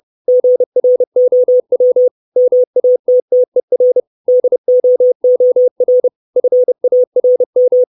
Morse code, heard as a keyed tone, about 30 wpm, 500 Hz. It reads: GROW MATTER DOOR FARM
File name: 632 Words 30WPM